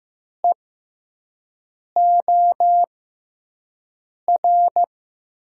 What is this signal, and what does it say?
Morse code, heard as a keyed tone, about 15 wpm, spaced as for 5 wpm, 700 Hz. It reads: EOR
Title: USA Call Signs - 6 Characters - 6x Character Spacing - 15wpm